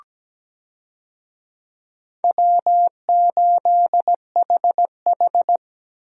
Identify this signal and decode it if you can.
Morse code, heard as a keyed tone, about 17 wpm, 700 Hz. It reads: W8HH
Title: USA Call Signs - 4 Characters - Repeated 3x - 17wpm